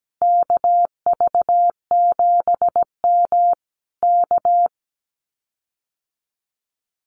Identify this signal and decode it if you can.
Morse code, heard as a keyed tone, about 17 wpm, 700 Hz. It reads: KV7M K
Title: Common QSO Phrases Encore 17wpm